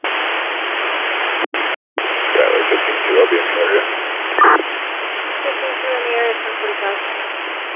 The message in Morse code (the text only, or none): none